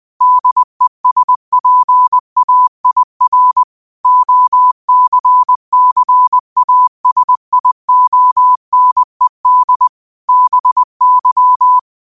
Morse code, heard as a keyed tone, about 20 wpm, 1000 Hz. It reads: DESPAIR OCCASIONED BY
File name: qotdfiles / 20wpm